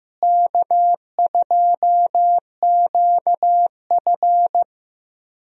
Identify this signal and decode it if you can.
Morse code, heard as a keyed tone, about 15 wpm, 700 Hz. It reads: K2QF
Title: USA Call Signs - 4 Characters - Repeated 2x - 15wpm